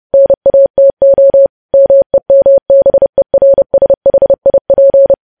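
Morse code, heard as a keyed tone, about 30 wpm, 550 Hz. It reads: NATO MEMBERSHIP
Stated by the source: News headlines sent in Morse code at 30.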